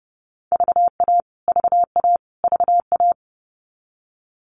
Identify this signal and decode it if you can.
Morse code, heard as a keyed tone, about 30 wpm, 700 Hz. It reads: VA VA VA
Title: US States Repeated 3x 30wpm